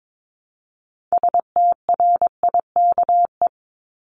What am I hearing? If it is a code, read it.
Morse code, heard as a keyed tone, about 22 wpm, 700 Hz. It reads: STRIKE
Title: Top 1501-1750 Words 22wpm